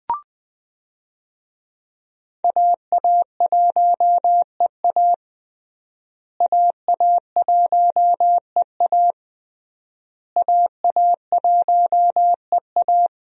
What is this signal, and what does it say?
Morse code, heard as a keyed tone, about 21 wpm, 700 Hz. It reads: AA1EA AA1EA AA1EA
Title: USA Call Signs - 5 Characters - Repeated 3x - 20wpm